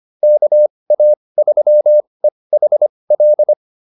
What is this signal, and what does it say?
Morse code, heard as a keyed tone, about 25 wpm, spaced as for 20 wpm, 600 Hz. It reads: KA3EHL